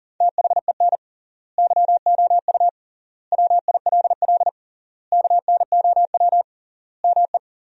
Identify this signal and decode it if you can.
Morse code, heard as a keyed tone, about 37 wpm, 700 Hz. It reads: THEN YOU WILL KNOW ME
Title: 3x Word-Spacing - Sentences from Top 400 Words - 40wpm